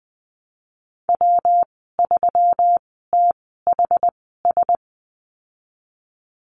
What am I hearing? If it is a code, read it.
Morse code, heard as a keyed tone, about 19 wpm, spaced as for 14 wpm, 700 Hz. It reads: W3THS
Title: 179 Lesson 29g - Slash - Call Signs 20wpm